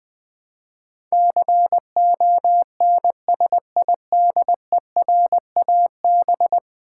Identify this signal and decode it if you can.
Morse code, heard as a keyed tone, about 20 wpm, 700 Hz. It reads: CONSIDERAB
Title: Build A Word - 5 Letters At A Time - 12-Letter Words - 20wpm